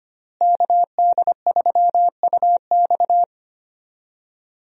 Morse code, heard as a keyed tone, about 25 wpm, 700 Hz. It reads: KD3UX
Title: USA Call Signs - 5 Characters - Repeated 3x - 25wpm